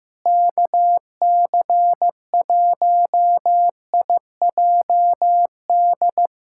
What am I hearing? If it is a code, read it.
Morse code, heard as a keyed tone, about 16 wpm, 700 Hz. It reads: KC1IJD
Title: USA Callsign DE Callsign - 6 Characters - 15wpm